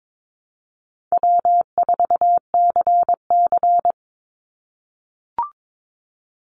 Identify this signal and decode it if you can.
Morse code, heard as a keyed tone, about 22 wpm, 700 Hz. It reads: W4CC
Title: Call Signs - 22wpm